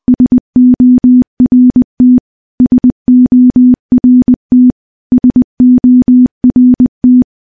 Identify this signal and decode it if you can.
Morse code, heard as a keyed tone, about 20 wpm, 250 Hz. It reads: SORT SORT SORT
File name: morse-training-20WPM